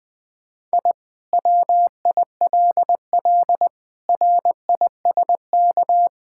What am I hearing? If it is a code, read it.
Morse code, heard as a keyed tone, about 20 wpm, 700 Hz. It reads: I WILL RISK
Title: Sentences from Top 1000 Words - Review Part 5 - 20wpm